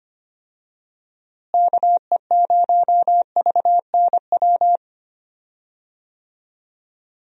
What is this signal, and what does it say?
Morse code, heard as a keyed tone, about 25 wpm, 700 Hz. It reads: KE0VNW
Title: USA Call Signs - 6 Characters - Repeated 3x - 25wpm